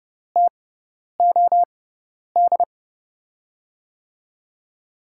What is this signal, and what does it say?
Morse code, heard as a keyed tone, about 29 wpm, spaced as for 10 wpm, 700 Hz. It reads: TOD
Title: USA Call Signs - 5 Characters - 6x Character Spacing - 30wpm